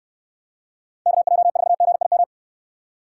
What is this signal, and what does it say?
Morse code, heard as a keyed tone, about 69 wpm, 700 Hz. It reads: KJ4GIM